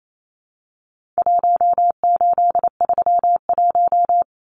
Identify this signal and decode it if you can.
Morse code, heard as a keyed tone, about 28 wpm, 700 Hz. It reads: 1831
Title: Numbers - Contest Style - 4-Digit 28wpm